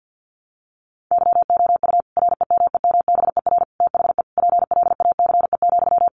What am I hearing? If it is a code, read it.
Morse code, heard as a keyed tone, about 50 wpm, 700 Hz. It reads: YOU REMEMBER THE PLACE,
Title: Sentences from Top 400 Words 50wpm